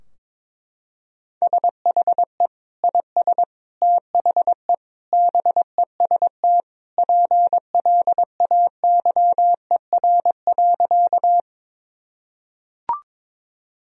Morse code, heard as a keyed tone, about 22 wpm, 700 Hz. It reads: SHE IS THE BEST PLAYER.
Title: Sentences from Top 1000 Words - Review Part 4 - 22wpm